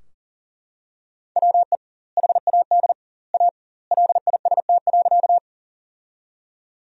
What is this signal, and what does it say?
Morse code, heard as a keyed tone, about 37 wpm, 700 Hz. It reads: WE HAD A LIST.
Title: 2x Word-Spacing - Sentences from Top 400 Words - 40wpm